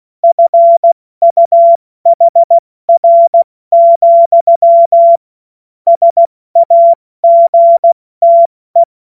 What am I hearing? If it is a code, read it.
Morse code, heard as a keyed tone, about 16 wpm, 680 Hz. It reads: FUHR, SAGTE
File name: Morsekurs / Witze / Witz 18. Mai 2021, 70 F80